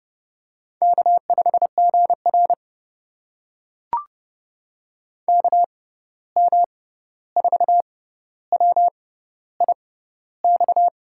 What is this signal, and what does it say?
Morse code, heard as a keyed tone, about 31 wpm, spaced as for 14 wpm, 700 Hz. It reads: K5GR KM4WIX
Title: USA Call Signs - 6x Character Spacing - 30wpm